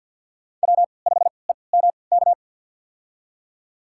Morse code, heard as a keyed tone, about 51 wpm, spaced as for 28 wpm, 690 Hz. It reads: W5EMK
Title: USA Call Signs - 5 Characters - 3x Character Spacing - 50wpm